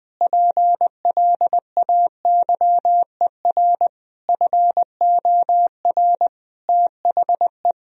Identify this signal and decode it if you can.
Morse code, heard as a keyed tone, about 20 wpm, 700 Hz. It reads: PLAYER FOR THE
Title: Sentences from Top 1000 Words - Review Part 4 - 20wpm